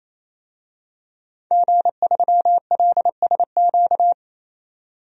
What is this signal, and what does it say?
Morse code, heard as a keyed tone, about 28 wpm, 700 Hz. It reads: G3LSQ
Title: DXCC-100 Call Signs - 5 Characters - 28wpm